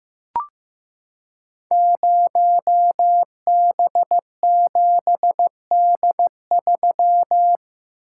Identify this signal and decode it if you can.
Morse code, heard as a keyed tone, about 15 wpm, 700 Hz. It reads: 0B7D3
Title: Code Groups - 5 Letter:Numbers - 15wpm